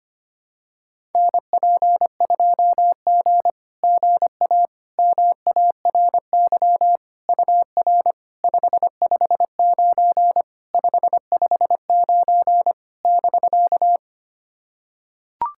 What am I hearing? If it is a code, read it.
Morse code, heard as a keyed tone, about 25 wpm, 690 Hz. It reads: NP2G GA MARY UR 559 559 <BK>
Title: SOTA QSO Practice 25wpm